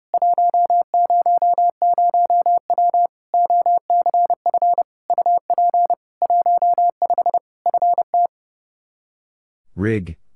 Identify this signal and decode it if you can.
Morse code, heard as a keyed tone, about 30 wpm, 700 Hz. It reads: 100W OCF UP 15 FT
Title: K1USN SST QSO Practice 30wpm